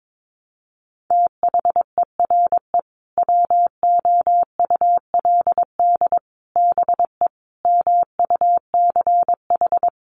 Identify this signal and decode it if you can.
Morse code, heard as a keyed tone, about 22 wpm, 700 Hz. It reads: THERE WOULD BE MUCH
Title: Sentences from Top 1200 Words 22wpm